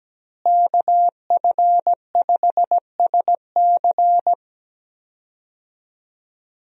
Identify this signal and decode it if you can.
Morse code, heard as a keyed tone, about 17 wpm, 700 Hz. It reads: KF5SC
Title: USA Call Signs - 5 Characters - Repeated 2x - 17wpm